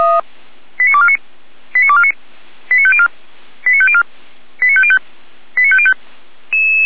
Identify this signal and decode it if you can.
Five-tone selective call (ZVEI), heard as quick runs of five tones, tones of about 70 ms. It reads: 89249 89249 89504 89504 89505 89505
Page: SDR/ZVEI